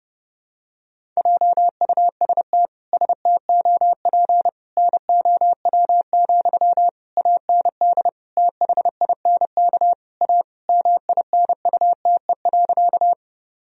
Morse code, heard as a keyed tone, about 30 wpm, 700 Hz. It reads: JUST STOP NOW, AND THINK A MINUTE.
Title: Sentences from Top 1000 Words - Review Part 1 - 30wpm